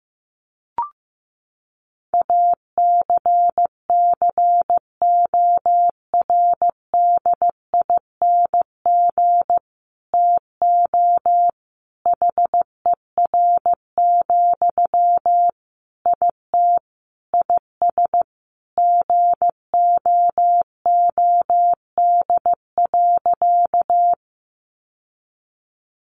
Morse code, heard as a keyed tone, about 14 wpm, 700 Hz. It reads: ACCORDING TO HER, IT IS GOOD.
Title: Sentences from Top 800 Words 15wpm